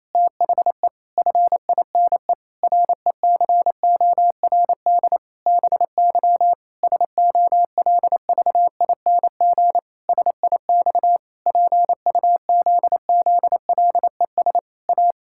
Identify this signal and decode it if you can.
Morse code, heard as a keyed tone, about 27 wpm, 700 Hz. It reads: THE FINE RECORD BY SOLVING SIX PUZZLES A
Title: Pangram Challenge v2 - 28wpm